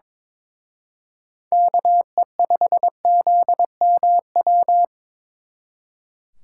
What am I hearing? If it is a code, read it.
Morse code, heard as a keyed tone, about 27 wpm, 700 Hz. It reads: KE5ZMW
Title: Speed Racing - USA 2x3 Call Signs 22wpm